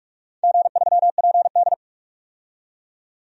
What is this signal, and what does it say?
Morse code, heard as a keyed tone, about 46 wpm, 700 Hz. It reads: G3PD
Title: DXCC-100 Call Signs - 4 Characters - 45wpm